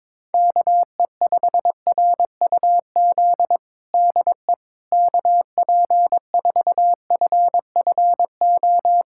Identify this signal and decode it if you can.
Morse code, heard as a keyed tone, about 22 wpm, 700 Hz. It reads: KE5RUZ DE KP4FFO
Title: USA Callsign DE Callsign - 6 Characters - 22wpm